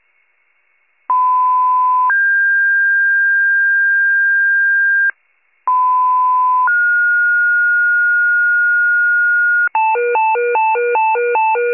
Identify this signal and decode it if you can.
Two-tone page, heard as a one-second tone, then a three-second tone, about 1010 then 1600 Hz, about 1010 then 1430 Hz, with noise. It is heard from 1.1 to 9.7 s.
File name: wcalert